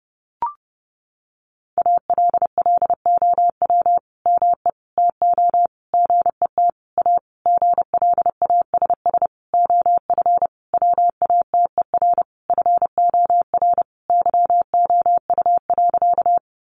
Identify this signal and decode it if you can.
Morse code, heard as a keyed tone, about 30 wpm, 700 Hz. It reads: ALLOW ME TO GET A GLASS OF WATER FOR YOU.
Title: Sentences from Top 1300 Words 30wpm